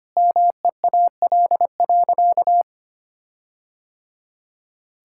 Morse code, heard as a keyed tone, about 25 wpm, 700 Hz. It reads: MEAL.
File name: practice / 25wpm / All-Age Appropriate Jokes - 25wpm